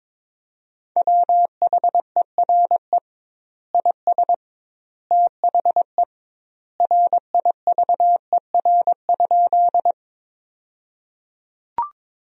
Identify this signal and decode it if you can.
Morse code, heard as a keyed tone, about 22 wpm, 700 Hz. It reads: WHERE IS THE RIVER?
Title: 2x Word-Spacing - Sentences from Top 300 Words - 22wpm